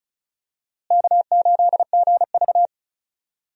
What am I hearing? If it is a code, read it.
Morse code, heard as a keyed tone, about 36 wpm, 700 Hz. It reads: K8GV
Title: USA Call Signs - 4 Characters - Repeated 2x - 35wpm